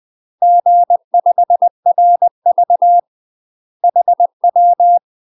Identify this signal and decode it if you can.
Morse code, heard as a keyed tone, about 20 wpm, 700 Hz. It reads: G5RV HW
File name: UK MASTER QSOs TRAINING / 20WPM / UK MASTER QSOs 0053